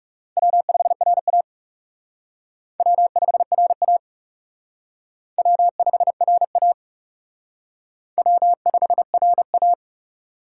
Morse code, heard as a keyed tone, about 37 wpm, 700 Hz. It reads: W5RA W5RA W5RA W5RA
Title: Speed Racing - USA 1x2 Call Signs 30wpm